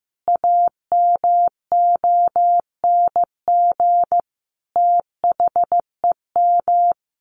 Morse code, heard as a keyed tone, about 15 wpm, 700 Hz. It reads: AMONG THEM
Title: Sentences from Top 1100 Words 15wpm